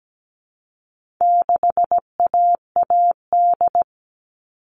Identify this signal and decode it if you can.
Morse code, heard as a keyed tone, about 17 wpm, 700 Hz. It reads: 6AAD
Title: All Cut Numbers - 4-Digit 17wpm